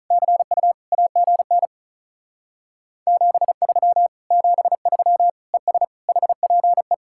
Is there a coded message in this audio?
Yes: Morse code from 0.1 to 6.9 s, heard as a keyed tone, about 46 wpm, 700 Hz. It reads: CU AGN 73 73 ES HPE